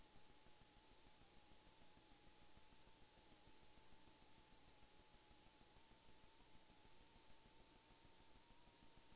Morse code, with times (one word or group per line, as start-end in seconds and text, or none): none